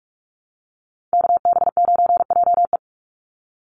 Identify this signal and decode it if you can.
Morse code, heard as a keyed tone, about 44 wpm, 700 Hz. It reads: KB9JE
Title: USA Call Signs 45wpm